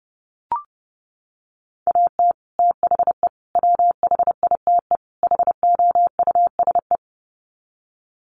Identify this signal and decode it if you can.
Morse code, heard as a keyed tone, about 30 wpm, 700 Hz. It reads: AT THE WHITE HOUSE
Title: Sets of 4 Words 30wpm